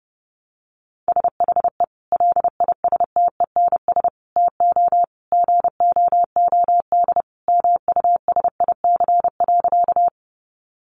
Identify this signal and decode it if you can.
Morse code, heard as a keyed tone, about 28 wpm, 700 Hz. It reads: SHE LISTENS TO GOOD MUSIC.
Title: Easy Phrases 30wpm